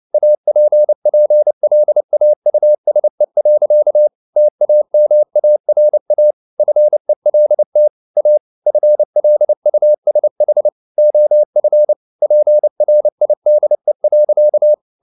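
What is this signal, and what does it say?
Morse code, heard as a keyed tone, about 29 wpm, 600 Hz. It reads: APPLAUSE. TAMARA FELT A FLUSH OF PRIDE.
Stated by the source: Morse code at 29 wpm, sending Ken Follett's novel Never in English, from CW med Gnister.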